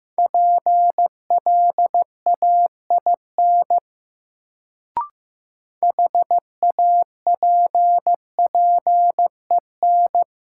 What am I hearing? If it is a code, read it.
Morse code, heard as a keyed tone, about 15 wpm, 700 Hz. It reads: PLAIN HAPPEN
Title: Sentences from Top 400 Words - New Word Drill - 15wpm